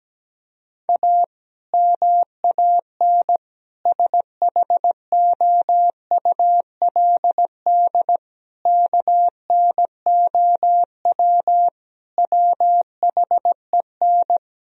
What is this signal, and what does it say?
Morse code, heard as a keyed tone, about 17 wpm, 700 Hz. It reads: A MAN SHOULD KNOW WHEN
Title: Sentences from Top 1000 Words - Review Part 2 - 17wpm